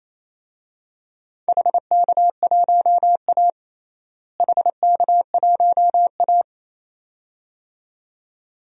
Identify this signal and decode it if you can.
Morse code, heard as a keyed tone, about 28 wpm, 700 Hz. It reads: HK1A HK1A
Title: DXCC-100 Call Signs - 4 Characters - Repeated 2x - 28wpm